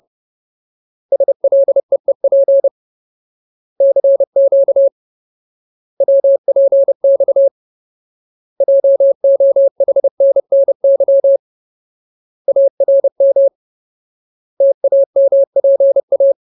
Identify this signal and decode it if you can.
Morse code, heard as a keyed tone, about 30 wpm, 550 Hz. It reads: SLEEP CQ WPX JOHNNY ARM TAMPA
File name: Ultimate Word List / 30WPM Ultimate